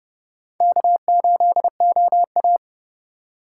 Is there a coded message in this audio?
Yes: Morse code from 0.6 to 2.6 s, heard as a keyed tone, about 26 wpm, 700 Hz. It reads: K8OA